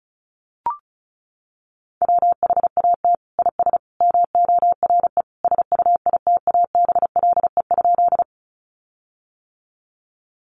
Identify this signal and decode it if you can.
Morse code, heard as a keyed tone, about 34 wpm, 700 Hz. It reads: WHAT IS MORE SUITABLE?
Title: Sentences from Top 1400 Words 35wpm